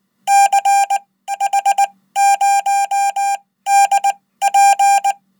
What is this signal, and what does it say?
Morse code, heard as a keyed tone, about 20 wpm, spaced as for 14 wpm, 800 Hz. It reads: CH0DP
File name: Code